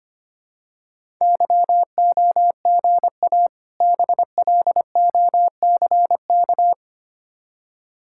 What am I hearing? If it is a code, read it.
Morse code, heard as a keyed tone, about 25 wpm, 700 Hz. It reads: YOGA BLOCK
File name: practice / 25wpm / Difficult Words with Hints - 25wpm